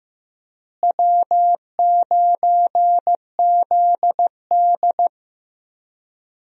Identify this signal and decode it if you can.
Morse code, heard as a keyed tone, about 15 wpm, 700 Hz. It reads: W9ZD
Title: USA Call Signs - 4 Characters - Repeated 2x - 15wpm